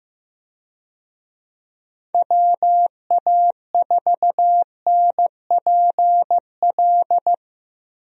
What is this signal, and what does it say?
Morse code, heard as a keyed tone, about 15 wpm, 700 Hz. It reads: WA4NPL
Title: USA Call Signs - 6 Characters - Repeated 3x - 15wpm